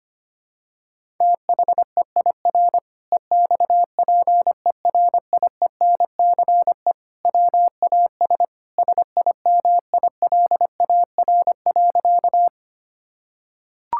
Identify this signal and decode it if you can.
Morse code, heard as a keyed tone, about 25 wpm, 700 Hz. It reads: THEIR EXPERIENCE WAS SIMILAR.
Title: Sentences from Top 1000 Words - Review Part 5 - 25wpm